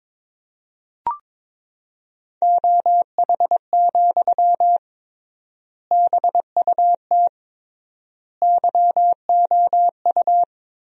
Morse code, heard as a keyed tone, about 22 wpm, 700 Hz. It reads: OH, BUT YOU
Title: 3x Word-Spacing - Sentences from Top 500 Words - 22wpm